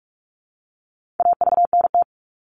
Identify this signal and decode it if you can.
Morse code, heard as a keyed tone, about 44 wpm, 700 Hz. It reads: AVNT
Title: All Cut Numbers - 4-Digit 45wpm